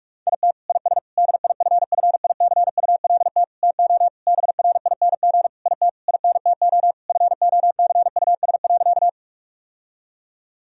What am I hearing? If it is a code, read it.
Morse code, heard as a keyed tone, about 46 wpm, 700 Hz. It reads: IT IS DIFFICULT TO BRING IT INTO FOCUS.
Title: Sentences from Top 1000 Words - Review Part 4 - 45wpm